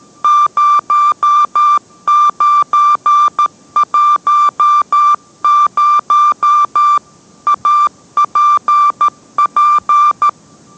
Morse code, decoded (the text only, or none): 0910 APP